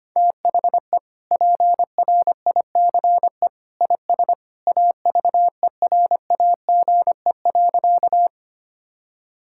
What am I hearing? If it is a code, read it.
Morse code, heard as a keyed tone, about 25 wpm, 700 Hz. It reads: THE PRICE IS AVERAGE.
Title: Sentences from Top 1000 Words - Review Part 6 - 25wpm